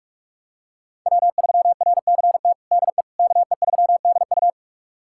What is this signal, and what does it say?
Morse code, heard as a keyed tone, about 47 wpm, 700 Hz. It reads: W3RCT DE KE3DU
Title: USA Callsign DE Callsign - 5 Characters - 45wpm